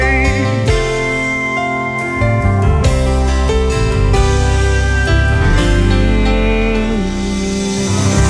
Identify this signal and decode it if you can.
Dial-up modem handshake.